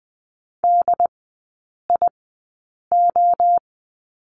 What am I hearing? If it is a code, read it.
Morse code, heard as a keyed tone, about 20 wpm, spaced as for 8 wpm, 700 Hz. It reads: DIO
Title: 081 Lesson 14e - Number 2 - Three Characters 20wpm